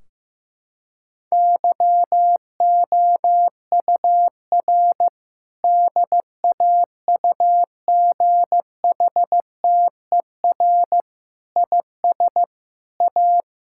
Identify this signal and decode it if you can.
Morse code, heard as a keyed tone, about 15 wpm, 700 Hz. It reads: YOUR DAUGHTER IS A
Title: Sentences from Top 1600 Words 15wpm